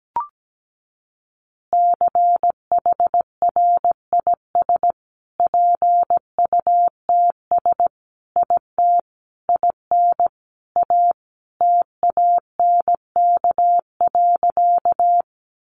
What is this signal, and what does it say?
Morse code, heard as a keyed tone, about 17 wpm, 700 Hz. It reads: CHRIS PUTS IT IN A TANK.
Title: Easy Phrases 17wpm